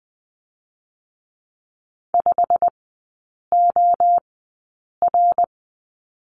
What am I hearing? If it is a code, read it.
Morse code, heard as a keyed tone, about 20 wpm, spaced as for 8 wpm, 690 Hz. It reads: 5OR
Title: 093 Lesson 16e - Letter c - Three Characters 20wpm